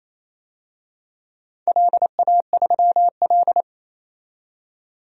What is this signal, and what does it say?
Morse code, heard as a keyed tone, about 28 wpm, 700 Hz. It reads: LA3L
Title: DXCC-100 Call Signs - 4 Characters - 28wpm